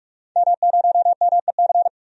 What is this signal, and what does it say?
Morse code, heard as a keyed tone, about 42 wpm, 700 Hz. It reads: M0MEC
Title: DXCC-100 Call Signs - 5 Characters - 45wpm